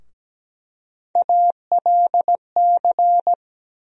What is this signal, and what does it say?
Morse code, heard as a keyed tone, about 17 wpm, 700 Hz. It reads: ALC
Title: USA Call Sign Postfixes - 3 Letters 17wpm